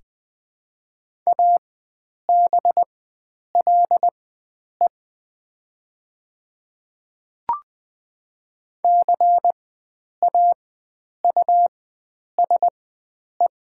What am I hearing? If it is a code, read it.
Morse code, heard as a keyed tone, about 20 wpm, spaced as for 9 wpm, 700 Hz. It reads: ABLE CAUSE
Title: 4x Character Spacing - Top 500 Words - 20wpm